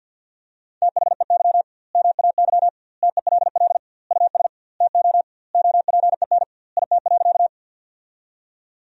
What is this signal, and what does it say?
Morse code, heard as a keyed tone, about 50 wpm, 700 Hz. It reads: THEY MAY TELL US TO OPEN IT.
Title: 2x Word-Spacing - Sentences from Top 300 Words - 50wpm